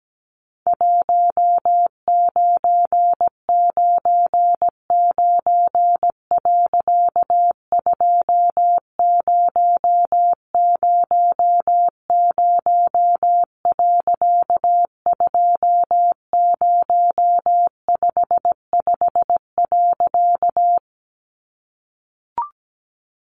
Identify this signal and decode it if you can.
Morse code, heard as a keyed tone, about 17 wpm, 700 Hz. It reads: 1999. 2000. 2055.
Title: A Sound Of Thunder 17wpm